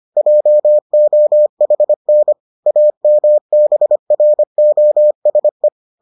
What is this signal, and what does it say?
Morse code, heard as a keyed tone, about 25 wpm, 600 Hz. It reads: JOHN AMBROSE